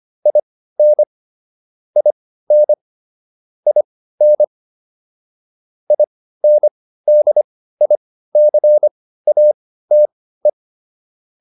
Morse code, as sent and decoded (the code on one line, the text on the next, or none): .. -. / .. -. / .. -. / .. -. -.. .. -.-. .- - .
IN IN IN INDICATE